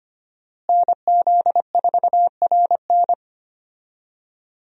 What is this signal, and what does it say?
Morse code, heard as a keyed tone, about 25 wpm, 700 Hz. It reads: NZ4RN
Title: USA Call Signs - 5 Characters - Repeated 2x - 25wpm